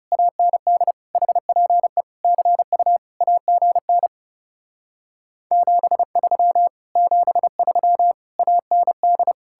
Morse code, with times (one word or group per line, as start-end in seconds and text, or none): 0.1-0.9 AND
1.2-2.0 HPE
2.2-3.0 CU
3.2-4.1 AGN
5.5-6.7 73
7.0-8.1 73
8.4-9.3 AND